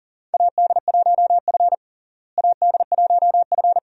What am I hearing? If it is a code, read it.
Morse code, heard as a keyed tone, about 40 wpm, 700 Hz. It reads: AD1F AD1F
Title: USA Call Signs - 4 Characters - Repeated 2x - 40wpm